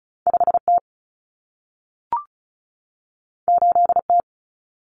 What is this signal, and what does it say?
Morse code, heard as a keyed tone, about 36 wpm, 700 Hz. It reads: 5T 8T
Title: Common Cut Numbers - 2-Digit 35wpm